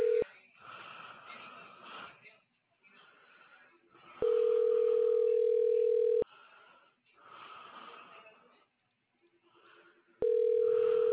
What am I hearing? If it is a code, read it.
Ringback tone (North America), 440 + 480 Hz.